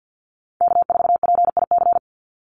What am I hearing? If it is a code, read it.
Morse code, heard as a keyed tone, about 48 wpm, 700 Hz. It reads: K4PIC
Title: NA SOTA Chaser Callsigns 50wpm